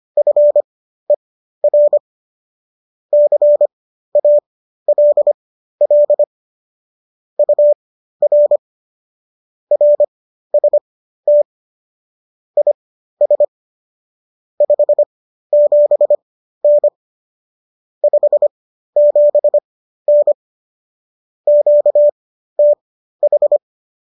Morse code, heard as a keyed tone, about 25 wpm, spaced as for 13 wpm, 590 Hz. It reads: FER CALL UR RST IS 57N 57N QTH